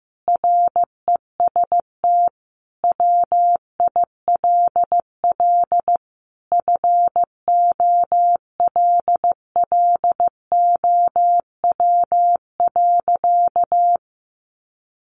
Morse code, heard as a keyed tone, about 15 wpm, 700 Hz. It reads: REST WILL FOLLOW.
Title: Sentences from Top 1500 Words 15wpm